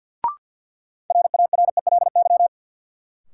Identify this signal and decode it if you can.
Morse code, heard as a keyed tone, about 50 wpm, 700 Hz. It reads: RARELY
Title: Sentences from Top 1400 Words - New Word Drill - 50wpm